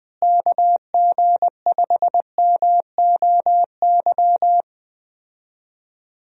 Morse code, heard as a keyed tone, about 21 wpm, 700 Hz. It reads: KG5MOY
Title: USA Call Signs - 6 Characters - Repeated 2x - 20wpm